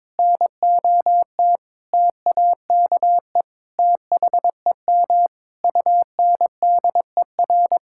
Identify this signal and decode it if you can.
Morse code, heard as a keyed tone, about 22 wpm, 700 Hz. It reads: NOT TAKE THEM UNDER
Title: Sentences from Top 1600 Words 22wpm